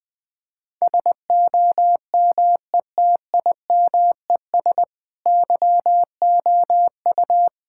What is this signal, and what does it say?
Morse code, heard as a keyed tone, about 21 wpm, 700 Hz. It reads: SOMETIMES YOU
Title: Sentences with 7-Word Phrases - Spoken Beginning - 20wpm